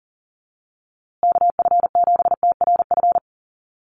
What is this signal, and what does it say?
Morse code, heard as a keyed tone, about 40 wpm, 700 Hz. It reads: KF7TRF
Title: USA Call Signs Repeated 2x 40wpm